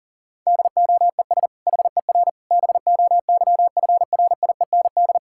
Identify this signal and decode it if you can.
Morse code, heard as a keyed tone, about 37 wpm, 700 Hz. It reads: DOES HER BOYFRIEND
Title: Sentences from Top 1000 Words - Review Part 5 - 40wpm